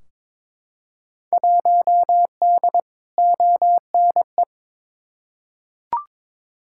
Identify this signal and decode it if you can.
Morse code, heard as a keyed tone, about 21 wpm, 700 Hz. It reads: 1D ONE
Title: Field Day Exchanges - 22wpm